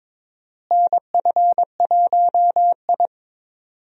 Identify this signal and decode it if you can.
Morse code, heard as a keyed tone, about 22 wpm, 700 Hz. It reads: NF1I
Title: USA Call Signs - 4 Characters - Repeated 2x - 22wpm